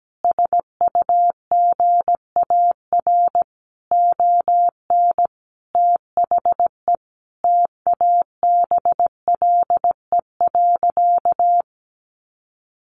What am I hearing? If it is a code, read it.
Morse code, heard as a keyed tone, about 17 wpm, 700 Hz. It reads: SUGAR ON THE TABLE.
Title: Sentences from Top 1200 Words 17wpm